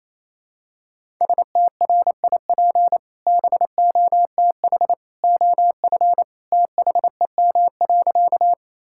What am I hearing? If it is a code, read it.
Morse code, heard as a keyed tone, about 28 wpm, 700 Hz. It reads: STRIP BOTH OF THEM.
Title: Sentences from Top 1600 Words 28wpm